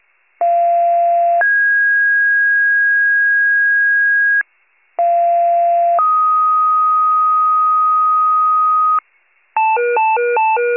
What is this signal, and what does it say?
Two-tone page, heard as a one-second tone, then a three-second tone, about 690 then 1650 Hz, about 690 then 1250 Hz, with noise. It is heard from 0.4 to 9.0 s.